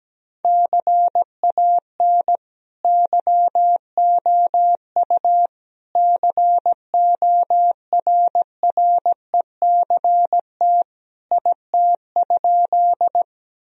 Morse code, heard as a keyed tone, about 17 wpm, 700 Hz. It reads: CAN YOU CORRECT IT?
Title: Sentences from Top 1000 Words - Review Part 2 - 17wpm